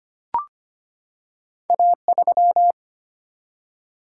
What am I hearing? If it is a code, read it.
Morse code, heard as a keyed tone, about 25 wpm, 700 Hz. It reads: A3
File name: Two Letter-Number Combinations - 25wpm